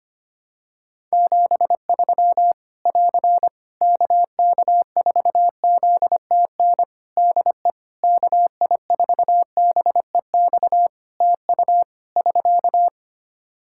Morse code, heard as a keyed tone, about 25 wpm, 700 Hz. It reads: 73 <AR> KK4ZTN DE KI4BEX TU <SK>
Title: K1USN SST QSO Practice 25wpm